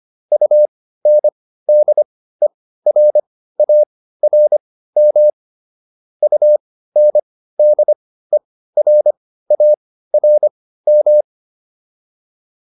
Morse code, heard as a keyed tone, about 25 wpm, spaced as for 15 wpm, 600 Hz. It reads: UNDERARM UNDERARM